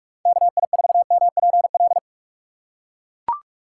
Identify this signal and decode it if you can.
Morse code, heard as a keyed tone, about 42 wpm, 700 Hz. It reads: KI4MPL
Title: USA Call Signs - 6 Characters - 6x Character Spacing - 45wpm